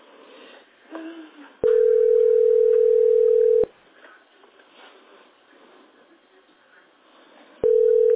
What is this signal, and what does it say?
Ringback tone (North America), 440 + 480 Hz.